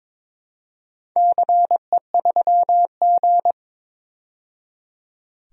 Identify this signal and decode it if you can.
Morse code, heard as a keyed tone, about 22 wpm, 700 Hz. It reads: CE3G
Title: DXCC-100 Call Signs - 4 Characters - 22wpm